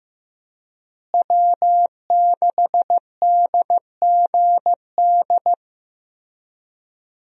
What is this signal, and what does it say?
Morse code, heard as a keyed tone, about 15 wpm, 700 Hz. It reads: W6DGD
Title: USA Call Signs - 5 Characters - Repeated 3x - 15wpm